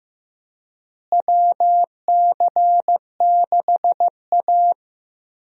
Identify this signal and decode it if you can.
Morse code, heard as a keyed tone, about 15 wpm, 700 Hz. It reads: WC6A
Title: USA Call Signs - 4 Characters - Repeated 3x - 15wpm